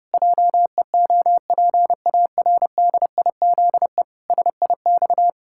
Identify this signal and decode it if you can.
Morse code, heard as a keyed tone, about 32 wpm, 700 Hz. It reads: JEOPARDIZE SIX
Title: Pangram Challenge v2 - 30wpm